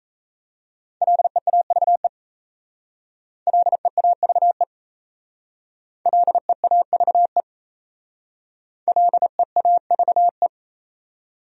Morse code, heard as a keyed tone, about 34 wpm, 700 Hz. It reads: LEAVE LEAVE LEAVE LEAVE
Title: Speed-Racing Top 500 Words 28wpm